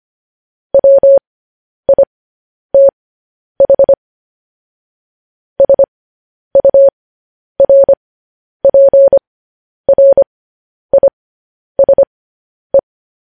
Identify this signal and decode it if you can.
Morse code, heard as a keyed tone, about 25 wpm, spaced as for 10 wpm, 550 Hz. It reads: WITH SURPRISE
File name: podcast / News headlines-1x-25x10